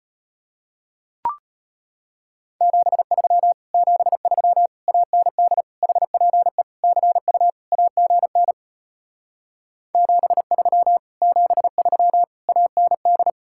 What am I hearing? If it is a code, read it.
Morse code, heard as a keyed tone, about 33 wpm, 700 Hz. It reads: 73 73 AND HPE CU AGN 73 73 AND